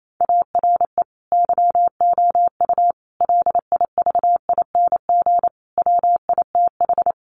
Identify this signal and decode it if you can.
Morse code, heard as a keyed tone, about 28 wpm, 700 Hz. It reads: ARE YOU LIVING WITH
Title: Sentences from Top 1100 Words 28wpm